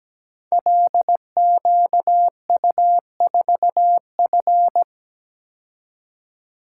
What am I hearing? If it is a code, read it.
Morse code, heard as a keyed tone, about 17 wpm, 700 Hz. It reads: LQU4F
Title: Code Groups - 5 Letter:Numbers - 17wpm